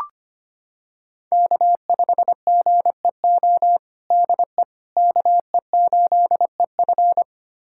Morse code, heard as a keyed tone, about 25 wpm, 700 Hz. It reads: K5GEO DE KE8EF
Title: USA Callsign DE Callsign - 5 Characters - 25wpm